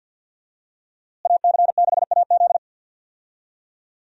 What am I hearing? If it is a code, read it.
Morse code, heard as a keyed tone, about 52 wpm, 700 Hz. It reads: AC6AZ
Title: USA Call Signs - 5 Characters - Repeated 3x - 50wpm